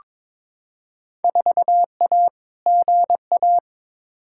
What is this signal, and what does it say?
Morse code, heard as a keyed tone, about 22 wpm, 700 Hz. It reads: 4A GA
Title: Field Day Exchanges - 22wpm